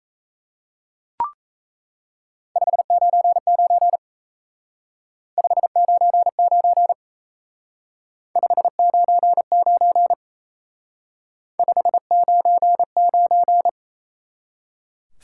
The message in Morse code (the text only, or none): 599 599 599 599